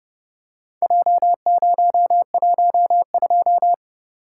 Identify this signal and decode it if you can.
Morse code, heard as a keyed tone, about 30 wpm, 700 Hz. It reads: J012